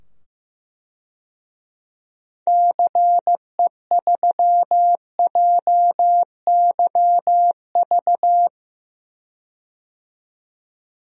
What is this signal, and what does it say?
Morse code, heard as a keyed tone, about 15 wpm, 700 Hz. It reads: CE3JYV
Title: DXCC-100 Call Signs - 6 Characters - 15wpm